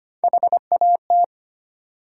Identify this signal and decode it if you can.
Morse code, heard as a keyed tone, about 25 wpm, 700 Hz. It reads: HAT
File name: practice / 25wpm / Easy Words with Hints - 25wpm